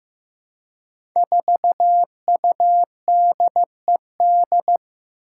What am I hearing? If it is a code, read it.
Morse code, heard as a keyed tone, about 15 wpm, 700 Hz. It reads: 4UDED
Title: Five Challenging Letter-Number Combinations 15wpm